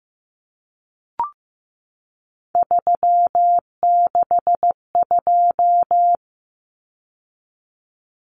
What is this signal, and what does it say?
Morse code, heard as a keyed tone, about 15 wpm, 700 Hz. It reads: 362
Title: Common Cut Numbers - 3-Digit 15wpm